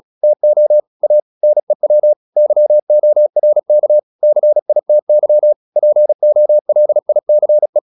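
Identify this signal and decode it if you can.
Morse code, heard as a keyed tone, about 36 wpm, 600 Hz. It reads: TO A NEW YORK CITY POLICE
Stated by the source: Morse practice at 36 wpm, text of Ken Follett's novel Never, from CW med Gnister.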